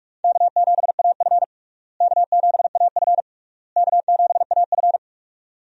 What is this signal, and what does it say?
Morse code, heard as a keyed tone, about 43 wpm, 700 Hz. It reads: K7AF K7AF K7AF
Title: USA Call Signs - 4 Characters - Repeated 3x - 45wpm